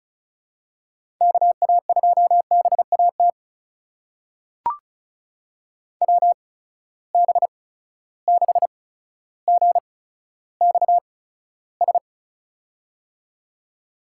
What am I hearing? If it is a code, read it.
Morse code, heard as a keyed tone, about 36 wpm, spaced as for 15 wpm, 700 Hz. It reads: KA2BAT WB6GXS
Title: USA Call Signs - 6 Characters - 8x Character Spacing - 35wpm